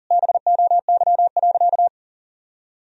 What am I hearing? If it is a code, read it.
Morse code, heard as a keyed tone, about 40 wpm, 700 Hz. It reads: BOY.
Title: Sentences from Top 1000 Words - Review Part 4 - 40wpm